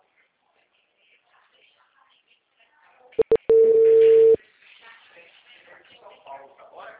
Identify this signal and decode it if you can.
Ringback tone (China, 450 Hz), with speech.